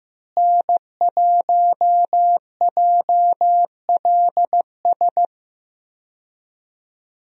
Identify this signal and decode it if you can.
Morse code, heard as a keyed tone, about 15 wpm, 700 Hz. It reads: N1JLS
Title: USA Call Signs - 5 Characters - Repeated 2x - 15wpm